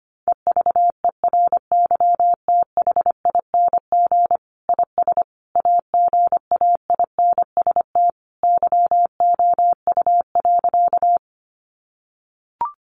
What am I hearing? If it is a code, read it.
Morse code, heard as a keyed tone, about 25 wpm, 700 Hz. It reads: EVERYTHING IS AGAINST YOU.
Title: Sentences from Top 700 Words 25wpm